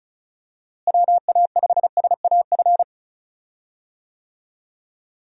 Morse code, heard as a keyed tone, about 36 wpm, 700 Hz. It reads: WA5SAF
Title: USA Call Signs - 6 Characters - Repeated 2x - 35wpm